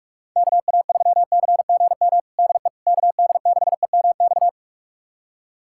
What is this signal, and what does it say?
Morse code, heard as a keyed tone, about 46 wpm, 690 Hz. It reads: KA3CGM DE KD6EMX
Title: USA Callsign DE Callsign 45wpm